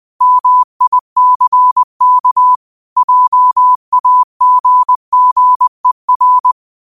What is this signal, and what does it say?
Morse code, heard as a keyed tone, about 20 wpm, 1000 Hz. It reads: MICK JAGGER